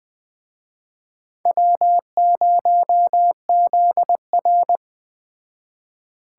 Morse code, heard as a keyed tone, about 21 wpm, 700 Hz. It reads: W0ZR